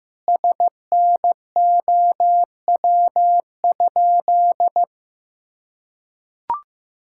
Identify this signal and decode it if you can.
Morse code, heard as a keyed tone, about 15 wpm, 700 Hz. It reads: SNOW?
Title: Sentences from Top 1600 Words 15wpm